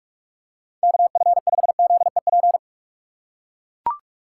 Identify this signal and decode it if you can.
Morse code, heard as a keyed tone, about 42 wpm, 700 Hz. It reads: KF5ZEP
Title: USA Call Signs - 6 Characters - 4x Character Spacing - 45wpm